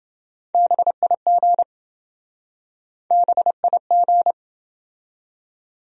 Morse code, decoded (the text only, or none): BIG BIG